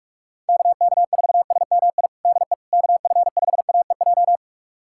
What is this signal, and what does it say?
Morse code, heard as a keyed tone, about 41 wpm, 700 Hz. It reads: KK4SMI DE KF5AEJ
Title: USA Callsign DE Callsign - 6 Characters - 45wpm